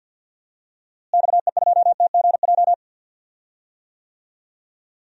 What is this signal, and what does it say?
Morse code, heard as a keyed tone, about 49 wpm, 700 Hz. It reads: XE2TGJ
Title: DXCC-100 Call Signs - 6 Characters - Repeated 2x - 50wpm